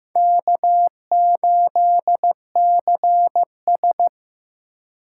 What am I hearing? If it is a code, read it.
Morse code, heard as a keyed tone, about 18 wpm, 700 Hz. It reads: K8CS
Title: Speed Racing - USA 1x2 Call Signs 15wpm